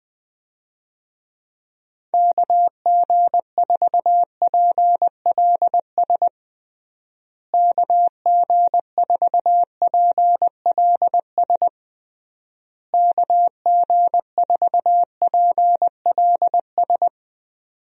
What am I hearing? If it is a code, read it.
Morse code, heard as a keyed tone, about 21 wpm, 700 Hz. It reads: KG4PLS KG4PLS KG4PLS
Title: USA Call Signs - 6 Characters - Repeated 3x - 20wpm